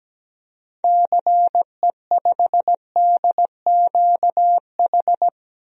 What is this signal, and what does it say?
Morse code, heard as a keyed tone, about 17 wpm, 700 Hz. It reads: CE5DQH
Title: DXCC-100 Call Signs - 6 Characters - 17wpm